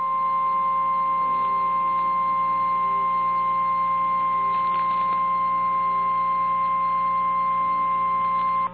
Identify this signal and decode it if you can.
Emergency Alert System (SAME) tones (the 1050 Hz attention tone), with noise. It is heard from 0.8 to 8.5 s.
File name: eas / KNCYKBIE / KNCYKBIE